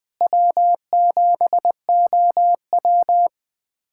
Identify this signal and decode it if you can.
Morse code, heard as a keyed tone, about 21 wpm, 700 Hz. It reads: W7OW
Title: USA Callsign DE Callsign - 4 Characters - 20wpm